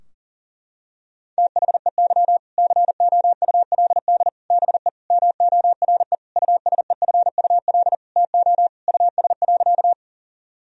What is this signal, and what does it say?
Morse code, heard as a keyed tone, about 40 wpm, 700 Hz. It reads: THEY COULD BE MORE USEFUL TO US.
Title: Sentences from Top 1000 Words - Review Part 4 - 40wpm